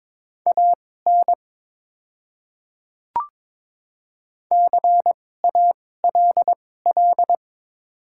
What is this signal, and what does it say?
Morse code, heard as a keyed tone, about 22 wpm, spaced as for 16 wpm, 690 Hz. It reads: AN CALL
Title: 2x Character Spacing - Top 100 Words - 22wpm